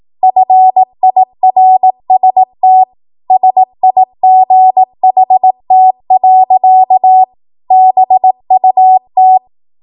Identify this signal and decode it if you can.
Morse code, heard as a keyed tone, about 18 wpm, 750 Hz. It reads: FIRST SIGHT. BUT
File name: Morse / 18WPM